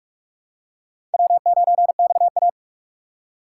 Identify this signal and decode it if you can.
Morse code, heard as a keyed tone, about 43 wpm, 700 Hz. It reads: W9XA